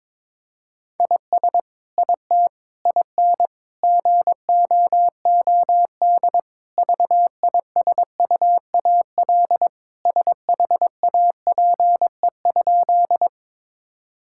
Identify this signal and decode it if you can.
Morse code, heard as a keyed tone, about 22 wpm, 700 Hz. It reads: IS IT IN GOOD VISUAL SHAPE?
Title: Sentences from Top 1000 Words - Review Part 6 - 22wpm